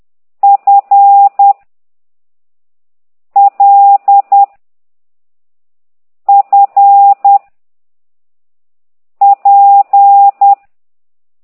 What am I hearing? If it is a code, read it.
Morse code, heard as a keyed tone, about 10 wpm, 800 Hz. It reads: F L F P
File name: ct2idw / exerc8